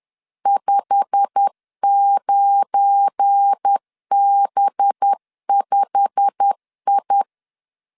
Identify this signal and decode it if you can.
Morse code, heard as a keyed tone, about 10 wpm, 800 Hz. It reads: 59B5I
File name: LEZIONI / 8 YJ45 / 8 YJ45 10wpm 2549